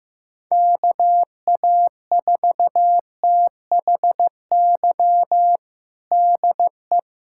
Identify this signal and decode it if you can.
Morse code, heard as a keyed tone, about 16 wpm, 700 Hz. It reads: KA4THY DE
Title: USA Callsign DE Callsign 15wpm